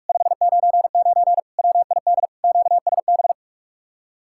Morse code, heard as a keyed tone, about 58 wpm, spaced as for 51 wpm, 700 Hz. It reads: 599 WID QSB
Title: Speed-Racing Common QSO Phrases - 45wpm